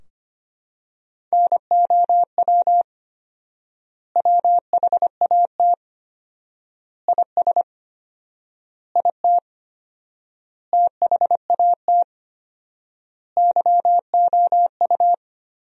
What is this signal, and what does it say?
Morse code, heard as a keyed tone, about 25 wpm, 700 Hz. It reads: NOW WHAT IS IT THAT YOU
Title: 4x Word-Spacing - Sentences from Top 200 Words - 25wpm